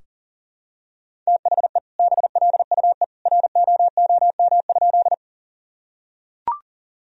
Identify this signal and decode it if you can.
Morse code, heard as a keyed tone, about 40 wpm, 700 Hz. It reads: THE BLUE ROOM?
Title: Sentences from Top 1000 Words - Review Part 3 - 40wpm